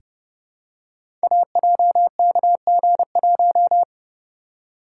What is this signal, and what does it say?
Morse code, heard as a keyed tone, about 30 wpm, 700 Hz. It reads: AJKG1
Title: Code Groups - 5 Letter:Numbers - 30wpm